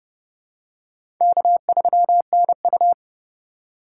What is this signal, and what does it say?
Morse code, heard as a keyed tone, about 31 wpm, 700 Hz. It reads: K3NU
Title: USA Call Signs - 4 Characters - 4x Character Spacing - 30wpm